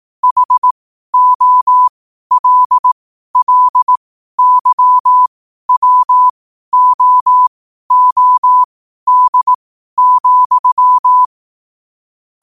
Morse code, heard as a keyed tone, about 18 wpm, spaced as for 12 wpm, 1000 Hz. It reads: HOLLYWOOD,
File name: qotdfiles / 12wpm